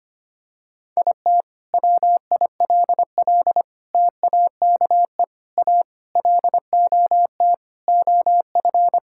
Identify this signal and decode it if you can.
Morse code, heard as a keyed tone, about 25 wpm, 700 Hz. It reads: IT WILL TAKE A LOT OF
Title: Sentences from Top 1000 Words - Review Part 6 - 25wpm